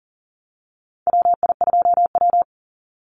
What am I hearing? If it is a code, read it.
Morse code, heard as a keyed tone, about 38 wpm, 700 Hz. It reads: WI2W
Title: NA SOTA Chaser Callsigns 40wpm